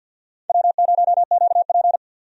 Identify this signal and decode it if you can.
Morse code, heard as a keyed tone, about 51 wpm, 700 Hz. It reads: W0QP
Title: USA Call Signs - 4 Characters - 50wpm